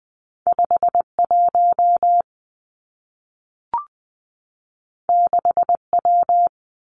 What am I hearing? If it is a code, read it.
Morse code, heard as a keyed tone, about 20 wpm, 700 Hz. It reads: 5 1 6 W
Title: Two Challenging Letter-Number Combinations 20wpm